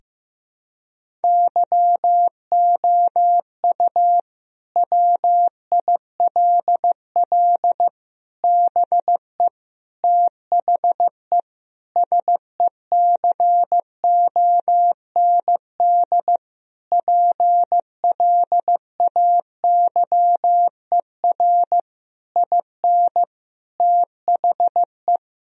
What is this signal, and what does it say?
Morse code, heard as a keyed tone, about 15 wpm, 700 Hz. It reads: YOU WILL BE THE SECOND PLAYER IN THE
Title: Sentences from Top 1000 Words - Review Part 4 - 15wpm